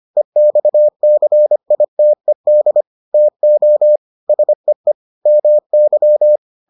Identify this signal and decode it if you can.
Morse code, heard as a keyed tone, about 25 wpm, 600 Hz. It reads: EXCITED TO SEE MY